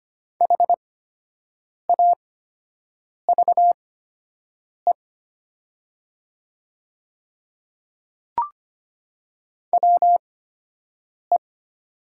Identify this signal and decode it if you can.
Morse code, heard as a keyed tone, about 25 wpm, spaced as for 7 wpm, 700 Hz. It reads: HAVE WE
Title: 8x Character Spacing - Top 100 Words - 25wpm